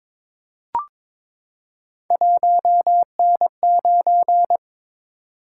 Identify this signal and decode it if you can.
Morse code, heard as a keyed tone, about 22 wpm, 700 Hz. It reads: 1N9
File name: practice / 22wpm / Three Letter-Number Combinations - 22wpm